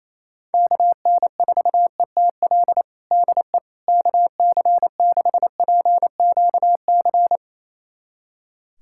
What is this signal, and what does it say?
Morse code, heard as a keyed tone, about 28 wpm, 700 Hz. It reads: KN4ETL DE KC6PQC
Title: USA Callsign DE Callsign 28wpm